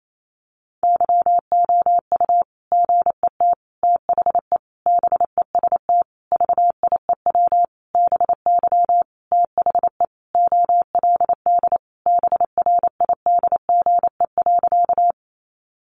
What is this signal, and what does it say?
Morse code, heard as a keyed tone, about 28 wpm, 700 Hz. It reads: YOU GET THE BEST VIEW BY THE OLD BRIDGE.
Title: Sentences from Top 1200 Words 28wpm